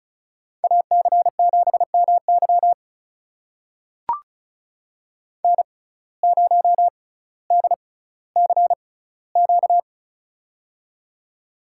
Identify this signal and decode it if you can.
Morse code, heard as a keyed tone, about 36 wpm, spaced as for 18 wpm, 700 Hz. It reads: AC7MY N0DCQ
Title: USA Call Signs - 5 Characters - 6x Character Spacing - 35wpm